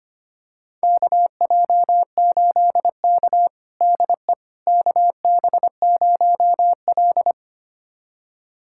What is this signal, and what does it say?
Morse code, heard as a keyed tone, about 25 wpm, 700 Hz. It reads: KJ8K DE KB0L
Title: USA Callsign DE Callsign - 4 Characters - 25wpm